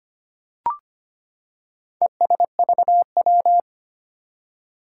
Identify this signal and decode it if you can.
Morse code, heard as a keyed tone, about 25 wpm, 700 Hz. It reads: ESVW
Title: Four Challenging Letter-Number Combinations 25wpm